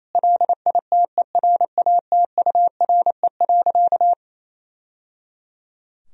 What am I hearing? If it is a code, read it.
Morse code, heard as a keyed tone, about 27 wpm, 700 Hz. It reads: LITERATURE.
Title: Sentences from Top 1000 Words - Review Part 3 - 28wpm